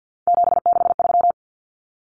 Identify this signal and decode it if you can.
Morse code, heard as a keyed tone, about 50 wpm, 700 Hz. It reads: 763
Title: Numbers - 3-Digit 50wpm